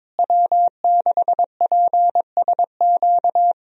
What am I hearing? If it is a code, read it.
Morse code, heard as a keyed tone, about 22 wpm, 700 Hz. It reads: W6PSQ
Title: USA Call Signs - 5 Characters - Repeated 3x - 22wpm